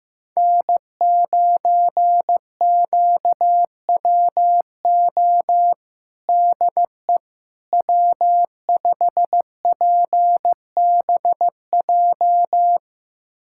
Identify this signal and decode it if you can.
Morse code, heard as a keyed tone, about 15 wpm, 700 Hz. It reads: N9QWO DE W5PBJ
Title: USA Callsign DE Callsign - 5 Characters - 15wpm